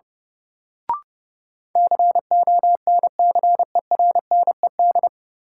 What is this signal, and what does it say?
Morse code, heard as a keyed tone, about 32 wpm, 700 Hz. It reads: CONCERNED
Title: Sentences from Top 1000 Words - Review Part 6 - New Word Drill - 30wpm